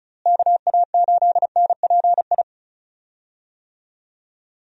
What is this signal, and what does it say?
Morse code, heard as a keyed tone, about 36 wpm, 700 Hz. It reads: KA8NPI
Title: USA Call Signs - 6 Characters - Repeated 2x - 35wpm